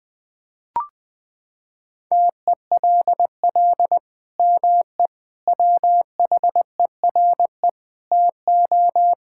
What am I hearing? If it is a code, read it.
Morse code, heard as a keyed tone, about 20 wpm, 700 Hz. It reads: TELL ME WHERE TO
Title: Sentences from Top 1600 Words 20wpm